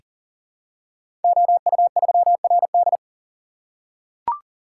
Morse code, heard as a keyed tone, about 39 wpm, 690 Hz. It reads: OU3RD
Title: Code Groups - 5 Letter:Numbers - 40wpm